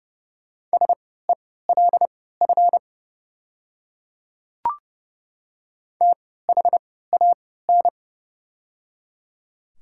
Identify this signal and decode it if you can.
Morse code, heard as a keyed tone, about 31 wpm, spaced as for 17 wpm, 700 Hz. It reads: SELF THAN
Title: 3x Character Spacing - Top 200 Words - 30wpm